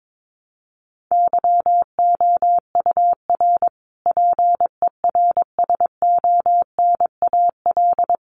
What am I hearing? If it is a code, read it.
Morse code, heard as a keyed tone, about 22 wpm, 700 Hz. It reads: YOUR PERSONAL
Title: A Sound Of Thunder 22wpm